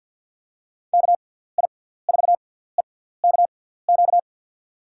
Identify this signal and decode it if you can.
Morse code, heard as a keyed tone, about 52 wpm, spaced as for 17 wpm, 700 Hz. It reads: KI4EKQ
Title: USA Call Signs - 6 Characters - 6x Character Spacing - 50wpm